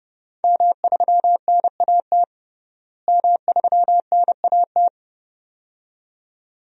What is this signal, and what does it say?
Morse code, heard as a keyed tone, about 30 wpm, 700 Hz. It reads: M3NAT M3NAT
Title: DXCC-100 Call Signs - 5 Characters - Repeated 2x - 30wpm